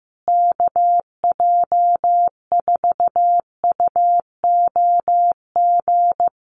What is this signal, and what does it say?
Morse code, heard as a keyed tone, about 15 wpm, 700 Hz. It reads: KJ4UOG
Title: USA Call Signs Repeated 3x 15wpm